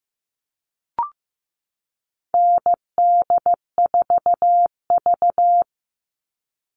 Morse code, heard as a keyed tone, about 15 wpm, 700 Hz. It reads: ND4V
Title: All Cut Numbers - 4-Digit 15wpm